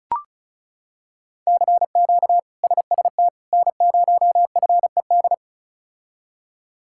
Morse code, heard as a keyed tone, about 34 wpm, 700 Hz. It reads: CQ SST N0FED
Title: K1USN SST QSO Practice - 35wpm